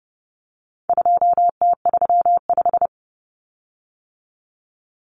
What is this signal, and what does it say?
Morse code, heard as a keyed tone, about 30 wpm, 700 Hz. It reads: 2T35
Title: Common Cut Numbers - 4-Digit 30wpm